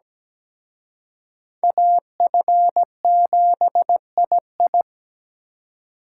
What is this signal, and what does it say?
Morse code, heard as a keyed tone, about 20 wpm, 700 Hz. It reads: AF7II
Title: Speed Racing - USA 2x2 Call Signs 17wpm